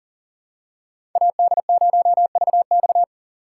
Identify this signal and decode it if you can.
Morse code, heard as a keyed tone, about 40 wpm, 700 Hz. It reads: AD0VX
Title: USA Call Signs - 2x Character Spacing - 40wpm